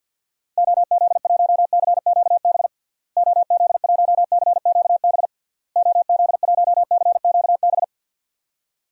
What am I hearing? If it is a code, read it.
Morse code, heard as a keyed tone, about 52 wpm, 700 Hz. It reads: OZ1CQB OZ1CQB OZ1CQB
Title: DXCC-100 Call Signs - 6 Characters - Repeated 3x - 50wpm